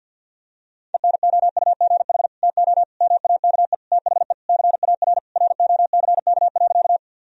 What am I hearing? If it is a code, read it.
Morse code, heard as a keyed tone, about 50 wpm, 700 Hz. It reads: ENOUGH TO MAKE THE CAR ROCK.
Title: Sentences from Top 1000 Words - Review Part 1 - 50wpm